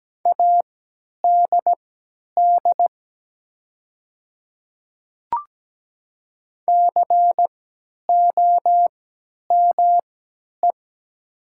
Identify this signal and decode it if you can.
Morse code, heard as a keyed tone, about 17 wpm, spaced as for 10 wpm, 700 Hz. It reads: ADD COME
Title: 3x Character Spacing - Top 200 Words - 17wpm